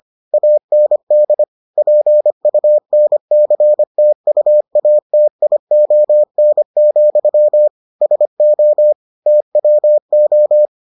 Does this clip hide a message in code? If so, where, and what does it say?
Yes: Morse code from 0.3 to 10.7 s, heard as a keyed tone, about 25 wpm, 600 Hz. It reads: AND PUNCTUATION, SO TWO